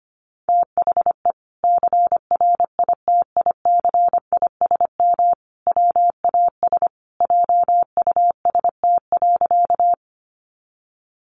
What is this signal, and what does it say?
Morse code, heard as a keyed tone, about 25 wpm, 700 Hz. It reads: THE CRITICISM WAS JUST.
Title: Sentences from Top 1300 Words 25wpm